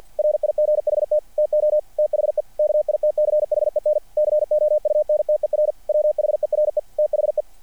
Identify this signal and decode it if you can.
Morse code, heard as a keyed tone, about 50 wpm, 600 Hz. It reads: RIGHT TO THE KITCHEN COUNTER WHERE THE
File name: archived / CRSnet